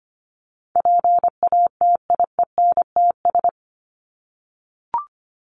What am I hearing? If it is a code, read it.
Morse code, heard as a keyed tone, about 25 wpm, 700 Hz. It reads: PATIENTS
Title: ES:S Suffix - Top 100 Words with Base Words 25wpm